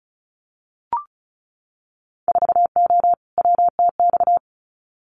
Morse code, heard as a keyed tone, about 36 wpm, 700 Hz. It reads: 4O WTX
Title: Winter Field Day Exchanges Encore 35wpm